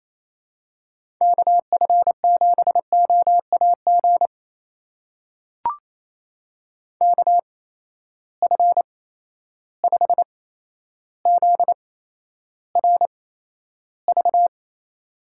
Morse code, heard as a keyed tone, about 28 wpm, spaced as for 12 wpm, 700 Hz. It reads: KF7OAG KF5ZRV
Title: USA Call Signs - 6 Characters - 8x Character Spacing - 28wpm